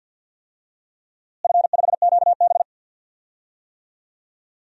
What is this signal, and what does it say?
Morse code, heard as a keyed tone, about 52 wpm, 700 Hz. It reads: F5QB
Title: DXCC-100 Call Signs - 4 Characters - Repeated 3x - 50wpm